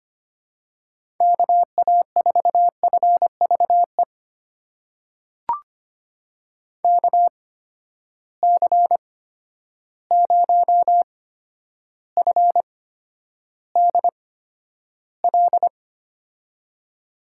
Morse code, heard as a keyed tone, about 25 wpm, spaced as for 11 wpm, 700 Hz. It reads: KA4FVE KC0FDL
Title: USA Call Signs - 6 Characters - 8x Character Spacing - 25wpm